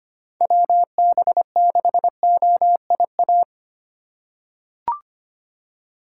Morse code, heard as a keyed tone, about 25 wpm, 700 Hz. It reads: WB6OIA
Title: USA Call Signs - 6 Characters - 4x Character Spacing - 25wpm